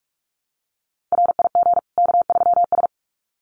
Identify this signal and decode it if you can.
Morse code, heard as a keyed tone, about 44 wpm, 700 Hz. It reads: RIG K3S
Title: Common QSO Phrases 45wpm